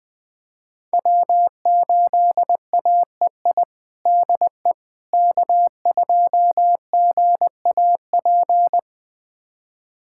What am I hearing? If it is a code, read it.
Morse code, heard as a keyed tone, about 20 wpm, 700 Hz. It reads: W8AEI DE K2GAP
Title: USA Callsign DE Callsign - 5 Characters - 20wpm